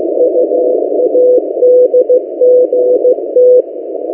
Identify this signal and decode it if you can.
Morse code, heard as a keyed tone, about 15 wpm, 500 Hz. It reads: UADGT
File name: M08a- -1355utc-8096khz